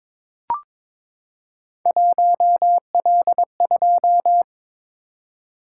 Morse code, heard as a keyed tone, about 23 wpm, 700 Hz. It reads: 1L2
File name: practice / 22wpm / Three Letter-Number Combinations - 22wpm